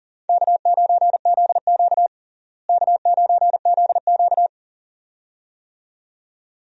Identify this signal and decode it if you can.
Morse code, heard as a keyed tone, about 40 wpm, 700 Hz. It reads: K9ZQ K9ZQ
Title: USA Call Signs - 4 Characters - Repeated 2x - 40wpm